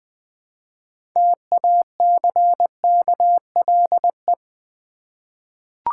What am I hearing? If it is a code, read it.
Morse code, heard as a keyed tone, about 20 wpm, 700 Hz. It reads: TACKLE
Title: Six Letter Words - 20wpm